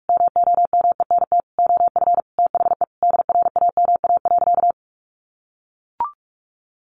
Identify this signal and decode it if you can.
Morse code, heard as a keyed tone, about 44 wpm, 700 Hz. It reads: MOMENT OF THE DRAMA.
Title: Sentences from Top 1500 Words 45wpm